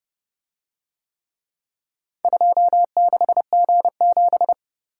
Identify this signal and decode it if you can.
Morse code, heard as a keyed tone, about 28 wpm, 700 Hz. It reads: 26G7